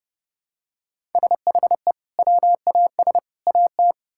Morse code, heard as a keyed tone, about 30 wpm, 700 Hz. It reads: SHE WAS AT
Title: Sentences from Top 1000 Words - Review Part 5 - 30wpm